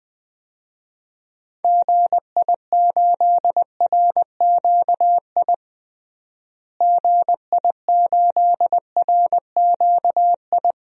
Morse code, heard as a keyed tone, about 20 wpm, 700 Hz. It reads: GI8RQI GI8RQI
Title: DXCC-100 Call Signs - 6 Characters - Repeated 2x - 20wpm